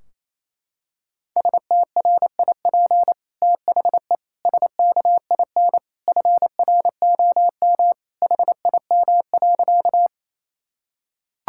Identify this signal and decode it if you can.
Morse code, heard as a keyed tone, about 28 wpm, 700 Hz. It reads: STRIP THE SKIN FROM HIM.
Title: Sentences from Top 1600 Words 28wpm